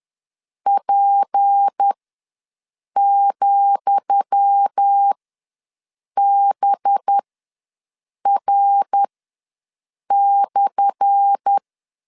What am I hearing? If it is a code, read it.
Morse code, heard as a keyed tone, about 10 wpm, spaced as for 6 wpm, 800 Hz. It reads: P,BR/
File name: LEZIONI / 10 SIGNS / 10D COMA ws 10wpm 4314